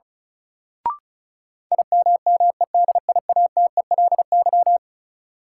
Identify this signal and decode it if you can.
Morse code, heard as a keyed tone, about 36 wpm, 700 Hz. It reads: IMMEDIATELY
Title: Sentences from Top 1100 Words - New Word Drill - 35wpm